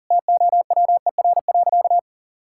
Morse code, heard as a keyed tone, about 40 wpm, 700 Hz. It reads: TOWER.
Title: Pangram Challenge v2 - 40wpm